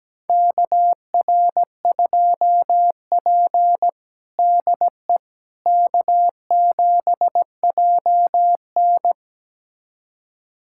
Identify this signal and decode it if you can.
Morse code, heard as a keyed tone, about 17 wpm, 700 Hz. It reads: KR2P DE K7JN
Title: USA Callsign DE Callsign - 4 Characters - 17wpm